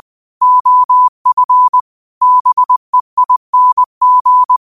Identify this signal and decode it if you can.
Morse code, heard as a keyed tone, about 20 wpm, 1000 Hz. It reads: OF BEING
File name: qotdfiles / 20wpm